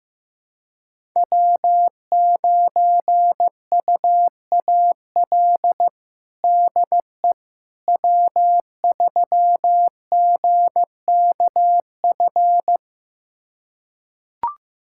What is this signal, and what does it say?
Morse code, heard as a keyed tone, about 16 wpm, 700 Hz. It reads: W9UAL DE W3GKF
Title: USA Callsign DE Callsign - 5 Characters - 15wpm